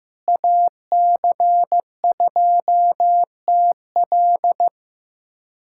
Morse code, heard as a keyed tone, about 15 wpm, 700 Hz. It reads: AC2TL
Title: USA Callsign DE Callsign - 5 Characters - 15wpm